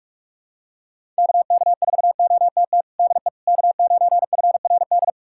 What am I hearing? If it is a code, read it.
Morse code, heard as a keyed tone, about 44 wpm, 690 Hz. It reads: KK4OTT DE K9FRD
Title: USA Callsign DE Callsign 45wpm